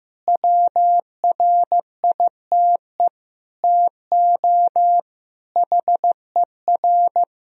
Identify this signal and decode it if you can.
Morse code, heard as a keyed tone, about 15 wpm, 700 Hz. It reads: WRITE TO HER
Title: Sentences from Top 1000 Words - Review Part 1 - 15wpm